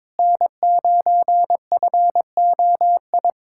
Morse code, heard as a keyed tone, about 22 wpm, 700 Hz. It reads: N9FOI
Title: USA Callsign DE Callsign - 5 Characters - 22wpm